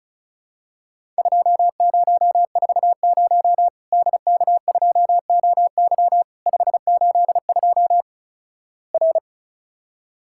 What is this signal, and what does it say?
Morse code, heard as a keyed tone, about 36 wpm, 700 Hz. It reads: 2040 DK2OY 582 R
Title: Worked All Europe DX Contest - QTC Exchange - 35wpm